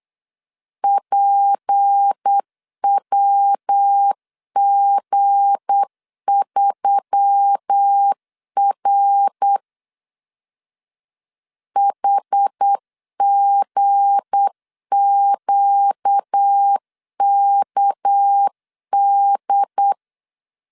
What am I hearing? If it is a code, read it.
Morse code, heard as a keyed tone, about 8 wpm, 800 Hz. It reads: PWG3R HGQKD
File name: LEZIONI / 7 GF9Z / 7 G 8wpm 3118